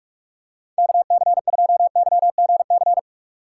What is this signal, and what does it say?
Morse code, heard as a keyed tone, about 42 wpm, 700 Hz. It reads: KC2YGC
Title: USA Call Signs - 6 Characters - 4x Character Spacing - 45wpm